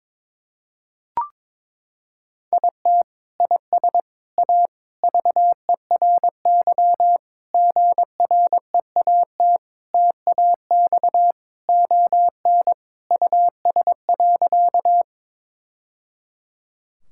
Morse code, heard as a keyed tone, about 22 wpm, 700 Hz. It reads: IT IS A VERY GREAT TAX ON US.
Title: Sentences from Top 1000 Words - Review Part 5 - 22wpm